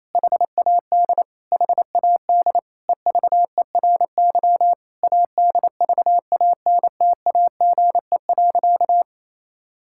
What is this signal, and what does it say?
Morse code, heard as a keyed tone, about 28 wpm, 700 Hz. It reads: HAD HAD EVERY ADVANTAGE.
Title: Sentences from Top 1000 Words - Review Part 6 - 28wpm